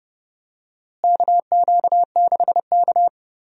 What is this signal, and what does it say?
Morse code, heard as a keyed tone, about 30 wpm, 700 Hz. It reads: KQ6K
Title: USA Call Signs - 4 Characters - Repeated 2x - 30wpm